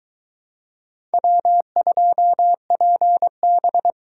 Morse code, heard as a keyed tone, about 22 wpm, 700 Hz. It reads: W2PB